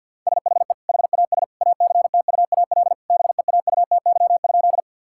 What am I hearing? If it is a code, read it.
Morse code, heard as a keyed tone, about 50 wpm, 700 Hz. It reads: SHE HAS ACTUAL BEAUTY?
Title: Sentences from Top 1000 Words - Review Part 5 - 50wpm